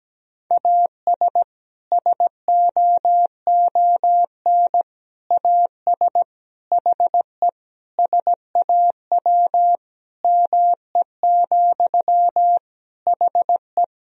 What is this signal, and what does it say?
Morse code, heard as a keyed tone, about 17 wpm, 700 Hz. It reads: AS SOON AS HE SAW ME, HE
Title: Sentences from Top 1000 Words - Review Part 2 - 17wpm